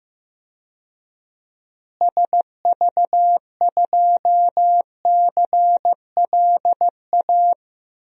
Morse code, heard as a keyed tone, about 15 wpm, 700 Hz. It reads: SV2CLA
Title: DXCC-100 Call Signs - 6 Characters - Repeated 2x - 15wpm